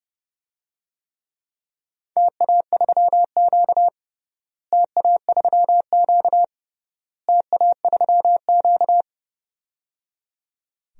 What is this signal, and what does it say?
Morse code, heard as a keyed tone, about 30 wpm, 700 Hz. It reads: TA3Q TA3Q TA3Q
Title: DXCC-100 Call Signs - 4 Characters - Repeated 3x - 30wpm